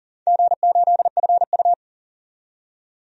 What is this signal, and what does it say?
Morse code, heard as a keyed tone, about 40 wpm, 700 Hz. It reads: G8FU
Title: DXCC-100 Call Signs - 4 Characters - 40wpm